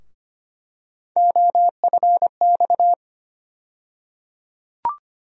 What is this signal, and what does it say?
Morse code, heard as a keyed tone, about 25 wpm, 700 Hz. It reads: OFX
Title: USA Call Sign Postfixes - 3 Letters 25wpm